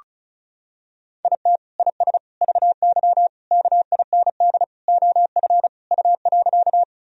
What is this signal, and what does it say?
Morse code, heard as a keyed tone, about 36 wpm, 700 Hz. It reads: IT IS VY KIND OF U.
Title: Sentences from Top 200 Words and Abbreviations 35wpm